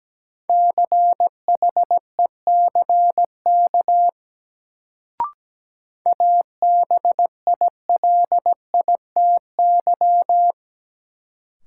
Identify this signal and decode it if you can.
Morse code, heard as a keyed tone, about 17 wpm, 700 Hz. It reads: CHECK ABILITY
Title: Sentences from Top 1000 Words - Review Part 3 - New Word Drill - 17wpm